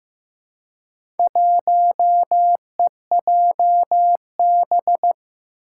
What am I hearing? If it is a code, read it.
Morse code, heard as a keyed tone, about 15 wpm, 700 Hz. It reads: 1EJB
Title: Four Challenging Letter-Number Combinations 15wpm